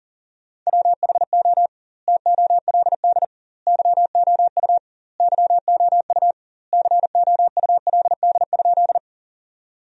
Morse code, heard as a keyed tone, about 40 wpm, 700 Hz. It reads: WHO TOLD YOU YOU COULD?
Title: 2x Word-Spacing - Sentences from Top 400 Words - 40wpm